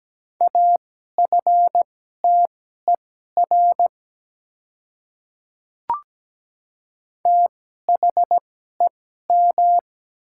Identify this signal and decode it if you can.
Morse code, heard as a keyed tone, about 17 wpm, spaced as for 12 wpm, 700 Hz. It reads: AFTER THEM
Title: 2x Character Spacing - Top 300 Words - 17wpm